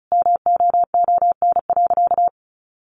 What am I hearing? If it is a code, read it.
Morse code, heard as a keyed tone, about 34 wpm, 700 Hz. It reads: MOON.
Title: Sentences from Top 1500 Words 35wpm